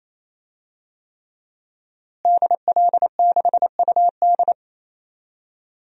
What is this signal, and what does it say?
Morse code, heard as a keyed tone, about 28 wpm, 700 Hz. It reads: DL6UD
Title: DXCC-100 Call Signs - 5 Characters - Repeated 3x - 28wpm